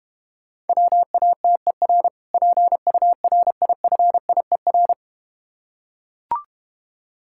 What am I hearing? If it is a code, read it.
Morse code, heard as a keyed tone, about 31 wpm, 700 Hz. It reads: WATER PURIFIER
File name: Easy Words - Speed-Racing with Hints - 25wpm